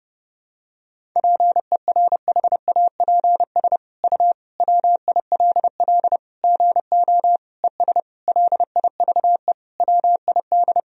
Morse code, heard as a keyed tone, about 30 wpm, 700 Hz. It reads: PERHAPS U WILL GO ES LIVE WID
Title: Sentences from Top 500 Words and Abbreviations 30wpm